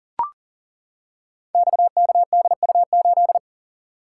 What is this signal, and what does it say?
Morse code, heard as a keyed tone, about 42 wpm, 700 Hz. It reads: XKDU8
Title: Code Groups - 5 Letter:Numbers - 40wpm